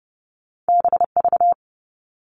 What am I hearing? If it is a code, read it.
Morse code, heard as a keyed tone, about 30 wpm, 700 Hz. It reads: BV
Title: All Cut Numbers - 2-Digit 30wpm